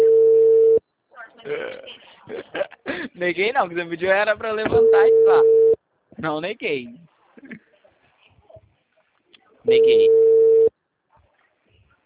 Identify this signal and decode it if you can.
Ringback tone (China, 450 Hz), with speech.
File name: out- -2009